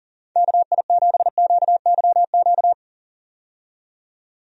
Morse code, heard as a keyed tone, about 40 wpm, 700 Hz. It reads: KI7QYQ
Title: USA Call Signs - 6 Characters - 40wpm